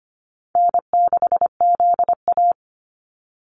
Morse code, heard as a keyed tone, about 25 wpm, 700 Hz. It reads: N6ZA
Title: NA SOTA Activator Callsigns 25wpm